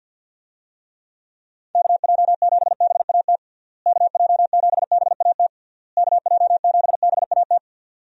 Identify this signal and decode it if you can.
Morse code, heard as a keyed tone, about 50 wpm, 700 Hz. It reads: KJ7BAT KJ7BAT KJ7BAT
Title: USA Call Signs - 6 Characters - Repeated 3x - 50wpm